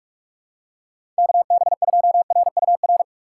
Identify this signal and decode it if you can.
Morse code, heard as a keyed tone, about 44 wpm, 700 Hz. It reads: KB2RUR
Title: USA Call Signs - 8x Character Spacing - 45wpm